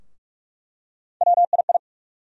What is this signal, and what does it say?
Morse code, heard as a keyed tone, about 42 wpm, 700 Hz. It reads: WII
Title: USA Call Sign Postfixes - 3 Letters 45wpm